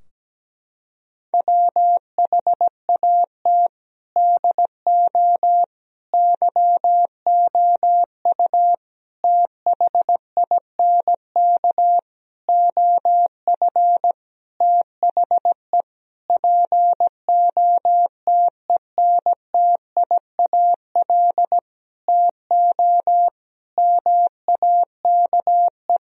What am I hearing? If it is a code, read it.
Morse code, heard as a keyed tone, about 17 wpm, 700 Hz. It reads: WHAT DO YOU THINK OF THE POTENTIAL TO MAKE
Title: Sentences from Top 1000 Words - Review Part 4 - 17wpm